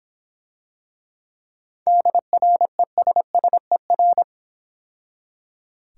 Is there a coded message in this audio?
Yes: Morse code from 1.9 to 4.2 s, heard as a keyed tone, about 26 wpm, 700 Hz. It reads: DRESSER